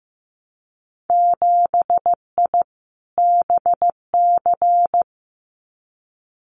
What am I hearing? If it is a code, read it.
Morse code, heard as a keyed tone, about 15 wpm, 700 Hz. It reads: 7I BC
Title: Winter Field Day Exchanges Encore 15wpm